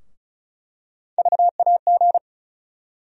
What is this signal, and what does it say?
Morse code, heard as a keyed tone, about 36 wpm, 700 Hz. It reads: VAG
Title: USA Call Sign Postfixes - 3 Letters 35wpm